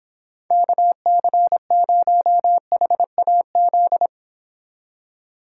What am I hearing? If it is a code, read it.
Morse code, heard as a keyed tone, about 26 wpm, 700 Hz. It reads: KC0HAZ